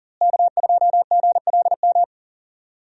Morse code, heard as a keyed tone, about 37 wpm, 700 Hz. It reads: K2GLM
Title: USA Call Signs - 5 Characters - Repeated 3x - 40wpm